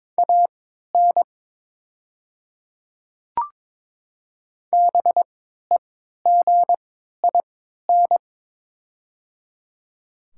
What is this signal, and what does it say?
Morse code, heard as a keyed tone, about 22 wpm, spaced as for 12 wpm, 700 Hz. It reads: AN BEGIN
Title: 3x Character Spacing - Top 500 Words - 22wpm